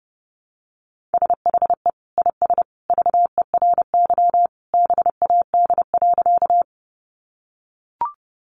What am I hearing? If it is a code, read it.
Morse code, heard as a keyed tone, about 30 wpm, 700 Hz. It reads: SHE IS VERY BAD.
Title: Sentences from Top 700 Words 30wpm